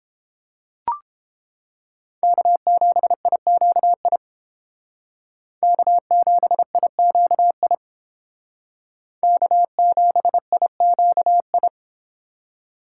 Morse code, heard as a keyed tone, about 29 wpm, 700 Hz. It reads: K7IQI K7IQI K7IQI